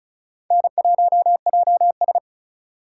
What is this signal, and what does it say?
Morse code, heard as a keyed tone, about 36 wpm, 700 Hz. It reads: N1JS
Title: USA Call Signs - 4 Characters - 2x Character Spacing - 35wpm